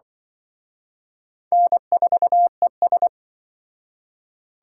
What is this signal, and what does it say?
Morse code, heard as a keyed tone, about 24 wpm, 700 Hz. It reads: N4ES
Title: Speed Racing - USA Call Signs 20wpm